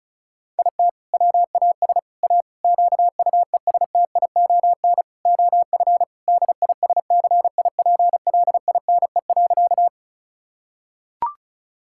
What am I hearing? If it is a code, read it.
Morse code, heard as a keyed tone, about 36 wpm, 700 Hz. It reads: IT WAS A QUESTION OF DISCIPLINE.
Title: Sentences from Top 1000 Words - Review Part 6 - 35wpm